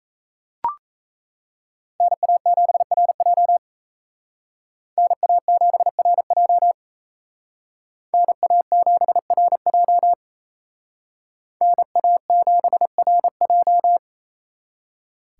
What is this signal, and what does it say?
Morse code, heard as a keyed tone, about 36 wpm, 700 Hz. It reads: NA7RJ NA7RJ NA7RJ NA7RJ